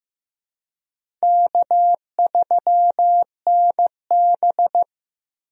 Morse code, heard as a keyed tone, about 15 wpm, 700 Hz. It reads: K3NB
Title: USA Call Signs - 4 Characters - 15wpm